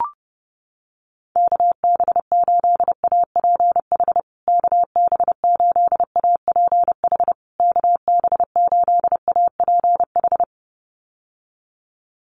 Morse code, heard as a keyed tone, about 30 wpm, 700 Hz. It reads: KB8APH KB8APH KB8APH
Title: USA Call Signs Repeated 3x 30wpm